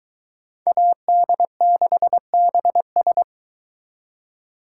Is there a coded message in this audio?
Yes: Morse code from 0.7 to 3.2 s, heard as a keyed tone, about 23 wpm, 700 Hz. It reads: AD6BS